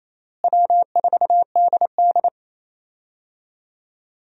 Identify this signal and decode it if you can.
Morse code, heard as a keyed tone, about 28 wpm, 700 Hz. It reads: W4DD
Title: USA Call Signs - 4 Characters - Repeated 3x - 28wpm